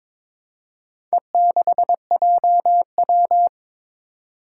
Morse code, heard as a keyed tone, about 22 wpm, 700 Hz. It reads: E6JW
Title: Four Challenging Letter-Number Combinations 22wpm